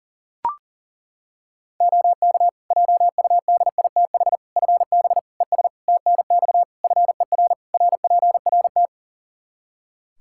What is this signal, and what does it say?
Morse code, heard as a keyed tone, about 37 wpm, 700 Hz. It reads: OK JUDITH FB ES TNX FER RPRT
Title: Practice QSO - Exchange 2 - Info - 40wpm